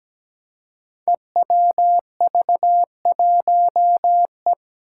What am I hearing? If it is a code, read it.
Morse code, heard as a keyed tone, about 17 wpm, 700 Hz. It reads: EWV1E
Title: Five Challenging Letter-Number Combinations 17wpm